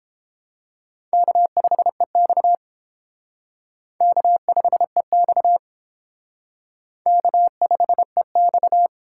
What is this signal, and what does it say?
Morse code, heard as a keyed tone, about 30 wpm, 700 Hz. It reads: K5EX K5EX K5EX